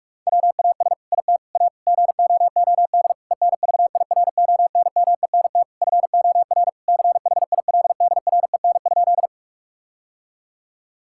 Morse code, heard as a keyed tone, about 46 wpm, 700 Hz. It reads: WAS IT A GOOD ENVIRONMENT FOR CHILDREN?
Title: Sentences from Top 1000 Words - Review Part 5 - 45wpm